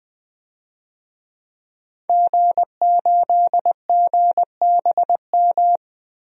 Morse code, heard as a keyed tone, about 20 wpm, 700 Hz. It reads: G8GBM
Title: DXCC-100 Call Signs - 5 Characters - Repeated 2x - 20wpm